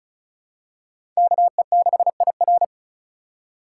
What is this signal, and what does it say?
Morse code, heard as a keyed tone, about 34 wpm, 700 Hz. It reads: KE6IR
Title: USA Call Signs - 2x Character Spacing - 35wpm